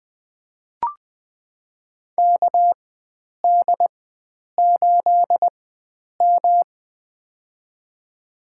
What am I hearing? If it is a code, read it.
Morse code, heard as a keyed tone, about 20 wpm, spaced as for 9 wpm, 700 Hz. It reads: KD8M
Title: USA Call Signs - 4 Characters - 4x Character Spacing - 20wpm